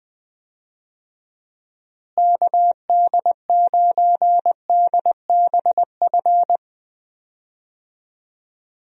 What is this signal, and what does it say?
Morse code, heard as a keyed tone, about 20 wpm, 700 Hz. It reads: KD9DBF
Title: USA Call Signs - 6 Characters - Repeated 2x - 20wpm